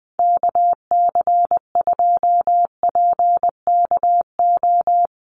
Morse code, heard as a keyed tone, about 20 wpm, 700 Hz. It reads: KC2PKO
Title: USA Call Signs 20wpm